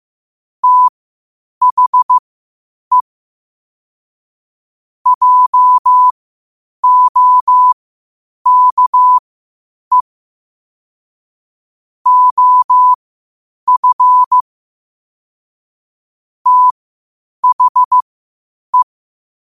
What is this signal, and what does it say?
Morse code, heard as a keyed tone, about 15 wpm, spaced as for 8 wpm, 1000 Hz. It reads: THE JOKE OF THE
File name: qotdfiles / 8wpm